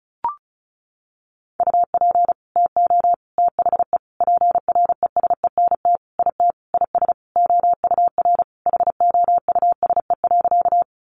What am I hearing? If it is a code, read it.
Morse code, heard as a keyed tone, about 34 wpm, 700 Hz. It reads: UP TO THE PRESENT IT IS OUR HOUSE.
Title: Sentences from Top 800 Words 35wpm